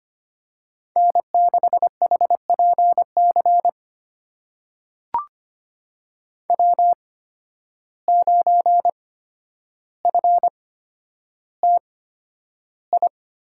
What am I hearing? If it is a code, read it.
Morse code, heard as a keyed tone, about 25 wpm, spaced as for 11 wpm, 700 Hz. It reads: N6HPC W9FTI
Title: USA Call Signs - 5 Characters - 8x Character Spacing - 25wpm